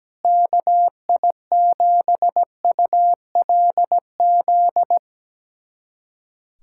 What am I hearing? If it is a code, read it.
Morse code, heard as a keyed tone, about 17 wpm, 700 Hz. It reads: KI7ULZ
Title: USA Callsign DE Callsign 17wpm